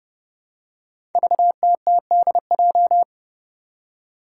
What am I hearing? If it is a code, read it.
Morse code, heard as a keyed tone, about 30 wpm, 700 Hz. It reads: VTTDJ
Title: Five Challenging Letter-Number Combinations 30wpm